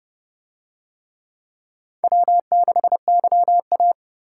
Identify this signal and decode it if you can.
Morse code, heard as a keyed tone, about 28 wpm, 700 Hz. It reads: W6YA